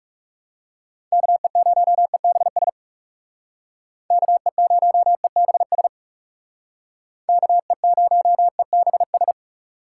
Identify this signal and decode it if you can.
Morse code, heard as a keyed tone, about 36 wpm, 700 Hz. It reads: KE0EBS KE0EBS KE0EBS
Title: Speed Racing - USA 2x3 Call Signs 30wpm